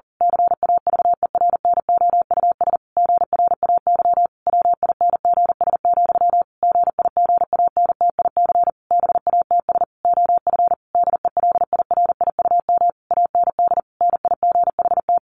Morse code, heard as a keyed tone, about 40 wpm, 700 Hz. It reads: CAVERNOUS GRAY WINGS, GIGANTIC BATS OF DELIRIUM AND NIGHT
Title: A Sound Of Thunder 40wpm